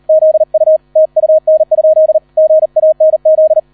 Morse code, heard as a keyed tone, about 38 wpm, 630 Hz. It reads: ZU TUN? GANZ